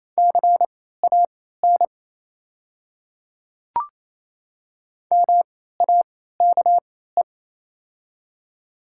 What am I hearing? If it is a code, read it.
Morse code, heard as a keyed tone, about 28 wpm, spaced as for 16 wpm, 700 Hz. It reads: CAN MAKE
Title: 3x Character Spacing - Top 100 Words - 28wpm